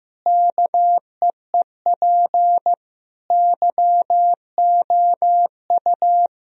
Morse code, heard as a keyed tone, about 15 wpm, 700 Hz. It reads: KEEP YOU
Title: Sentences from Top 1000 Words - Review Part 3 - 15wpm